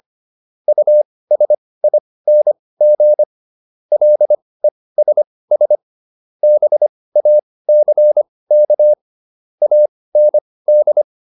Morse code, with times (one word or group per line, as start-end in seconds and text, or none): 0.7-3.2 USING
3.9-5.8 LESS
6.4-8.9 BACK
9.6-11.0 AND